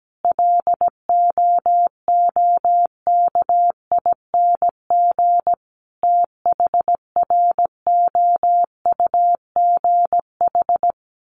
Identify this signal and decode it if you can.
Morse code, heard as a keyed tone, about 17 wpm, 700 Hz. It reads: LOOKING THROUGH
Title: A Sound Of Thunder 17wpm